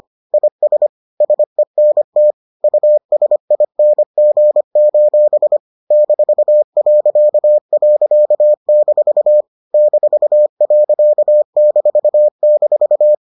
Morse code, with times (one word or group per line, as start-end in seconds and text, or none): 0.3-0.9 IS
1.2-2.3 SENT
2.6-5.6 USING:
5.9-9.4 -..-
9.7-13.1 -.--